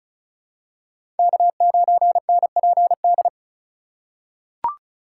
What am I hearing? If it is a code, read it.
Morse code, heard as a keyed tone, about 36 wpm, 700 Hz. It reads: K9NPD
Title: USA Call Signs - 5 Characters - 3x Character Spacing - 35wpm